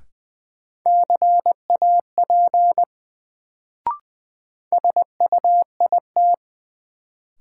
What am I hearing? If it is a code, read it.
Morse code, heard as a keyed tone, about 20 wpm, 700 Hz. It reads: CAP SUIT
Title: Sentences from Top 1600 Words - New Word Drill - 20wpm